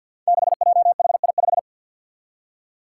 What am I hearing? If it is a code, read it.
Morse code, heard as a keyed tone, about 47 wpm, 700 Hz. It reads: 6JHI5
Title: Five Challenging Letter-Number Combinations 50wpm